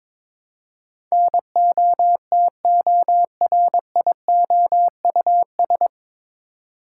Morse code, heard as a keyed tone, about 21 wpm, 700 Hz. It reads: NOTORIOUS
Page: Morse Code Ninja practice 22wpm